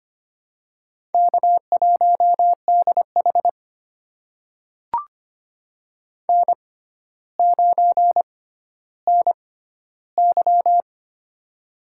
Morse code, heard as a keyed tone, about 25 wpm, spaced as for 13 wpm, 700 Hz. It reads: K1DH N9NY
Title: USA Call Signs - 4 Characters - 6x Character Spacing - 25wpm